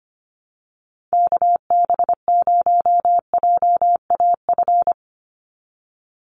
Morse code, heard as a keyed tone, about 25 wpm, 700 Hz. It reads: KB0JAF
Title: USA Call Signs 25wpm